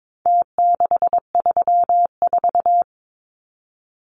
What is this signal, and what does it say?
Morse code, heard as a keyed tone, about 22 wpm, 700 Hz. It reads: T634
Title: Numbers - Contest Style - 4-Digit 22wpm